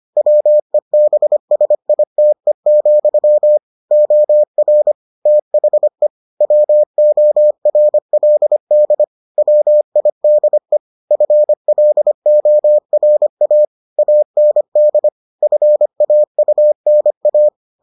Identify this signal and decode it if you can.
Morse code, heard as a keyed tone, about 25 wpm, 600 Hz. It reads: WEBSITE, OR THE WORLD WIDE FLORA AND FAUNA